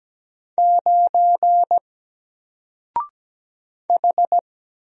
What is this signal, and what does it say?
Morse code, heard as a keyed tone, about 17 wpm, 700 Hz. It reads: 9 H
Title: Single Letter-Number Alt 17wpm